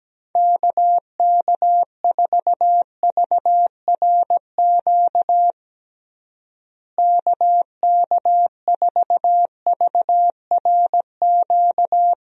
Morse code, heard as a keyed tone, about 17 wpm, 700 Hz. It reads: KK4VRQ KK4VRQ
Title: USA Call Signs - 6 Characters - Repeated 3x - 17wpm